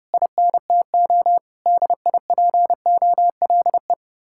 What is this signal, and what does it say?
Morse code, heard as a keyed tone, about 30 wpm, 700 Hz. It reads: INTO DIPOLE
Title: Speed-Racing Common QSO Phrases Encore - 25wpm